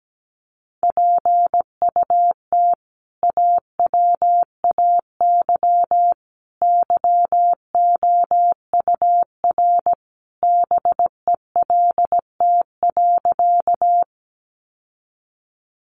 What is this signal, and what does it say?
Morse code, heard as a keyed tone, about 17 wpm, 700 Hz. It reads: PUT AWAY YOUR BELT.
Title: Sentences from Top 1400 Words 17wpm